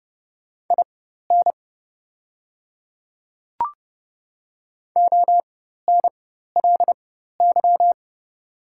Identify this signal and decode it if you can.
Morse code, heard as a keyed tone, about 30 wpm, spaced as for 14 wpm, 700 Hz. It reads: IN ONLY
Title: 4x Character Spacing - Top 200 Words - 30wpm